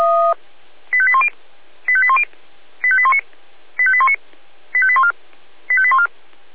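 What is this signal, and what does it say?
Five-tone selective call (ZVEI), heard as quick runs of five tones, tones of about 70 ms. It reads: 85810 85810 85819 85819 85814 85814
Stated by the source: Received on SDR.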